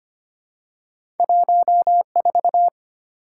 Morse code, heard as a keyed tone, about 25 wpm, 700 Hz. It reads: 14
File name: Two Letter-Number Combinations - 25wpm